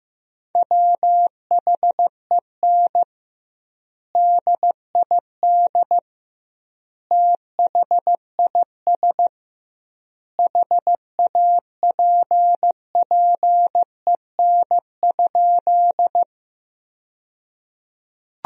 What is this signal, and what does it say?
Morse code, heard as a keyed tone, about 15 wpm, 700 Hz. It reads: WHEN DID THIS HAPPEN?
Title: 2x Word-Spacing - Sentences from Top 400 Words - 15wpm